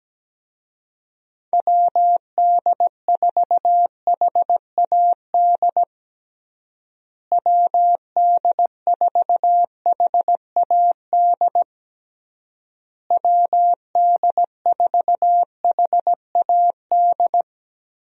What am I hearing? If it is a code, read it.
Morse code, heard as a keyed tone, about 17 wpm, 700 Hz. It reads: WD4HAD WD4HAD WD4HAD
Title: USA Call Signs - 6 Characters - Repeated 3x - 17wpm